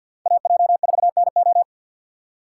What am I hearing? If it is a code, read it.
Morse code, heard as a keyed tone, about 62 wpm, 700 Hz. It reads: AJ4NO
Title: Speed Racing - USA 2x2 Call Signs 50wpm